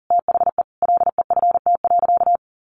Morse code, heard as a keyed tone, about 40 wpm, 700 Hz. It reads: THE LEFT.
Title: Sentences from Top 300 Words 40wpm